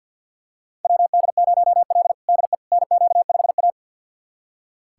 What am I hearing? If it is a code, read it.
Morse code, heard as a keyed tone, about 46 wpm, 700 Hz. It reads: WD0L DE NQ5A
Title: USA Callsign DE Callsign - 4 Characters - 50wpm